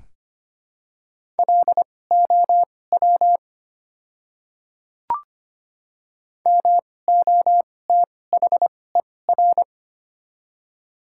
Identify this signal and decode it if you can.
Morse code, heard as a keyed tone, about 25 wpm, spaced as for 18 wpm, 700 Hz. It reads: LOW MOTHER
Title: 2x Character Spacing - Top 200 Words - 25wpm